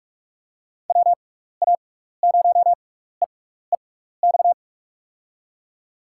Morse code, heard as a keyed tone, about 46 wpm, 700 Hz. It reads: W A 0 E E X
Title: USA Call Signs - 6 Characters - 6x Character Spacing - 45wpm